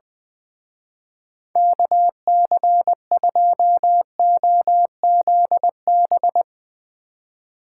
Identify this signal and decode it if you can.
Morse code, heard as a keyed tone, about 20 wpm, 700 Hz. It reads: KC2OZB
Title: Speed Racing - USA 2x3 Call Signs 20wpm